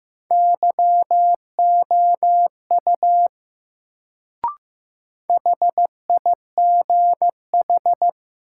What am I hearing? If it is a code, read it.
Morse code, heard as a keyed tone, about 15 wpm, 700 Hz. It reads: YOU HIGH
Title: Sets of 2 Words - New Word Drill - 15wpm